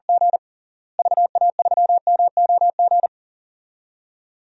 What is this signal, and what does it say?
Morse code, heard as a keyed tone, about 40 wpm, 700 Hz. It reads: G VA3MOG
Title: DXCC-100 Call Signs - 6 Characters - Repeated 2x - 40wpm